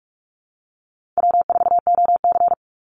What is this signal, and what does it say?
Morse code, heard as a keyed tone, about 44 wpm, 700 Hz. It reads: W4OC
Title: Easy Call Signs Repeated 2x - 45wpm